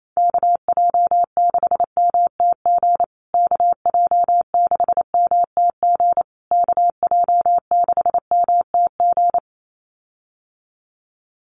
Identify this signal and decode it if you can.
Morse code, heard as a keyed tone, about 28 wpm, 700 Hz. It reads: KJ6MTG KJ6MTG KJ6MTG
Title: USA Call Signs Repeated 3x 28wpm